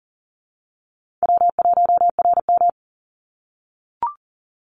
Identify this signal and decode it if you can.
Morse code, heard as a keyed tone, about 40 wpm, 700 Hz. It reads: W1RM
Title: Call Signs - 40wpm